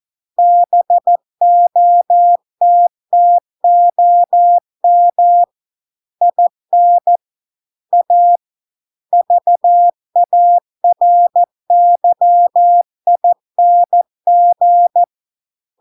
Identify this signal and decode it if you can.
Morse code, heard as a keyed tone, about 14 wpm, 700 Hz. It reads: BOTTOM IN A VARYING